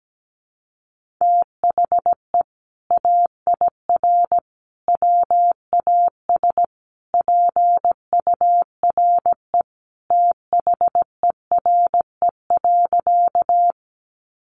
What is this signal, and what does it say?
Morse code, heard as a keyed tone, about 17 wpm, 700 Hz. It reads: THE AIR WAS PURE THERE.
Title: Sentences from Top 1500 Words 17wpm